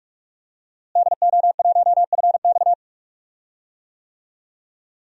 Morse code, heard as a keyed tone, about 46 wpm, 700 Hz. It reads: DO1FX
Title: DXCC-100 Call Signs - 5 Characters - Repeated 3x - 45wpm